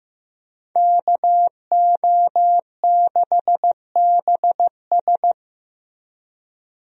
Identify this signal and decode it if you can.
Morse code, heard as a keyed tone, about 15 wpm, 700 Hz. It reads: KO6BS
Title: USA Call Signs - 5 Characters - Repeated 3x - 15wpm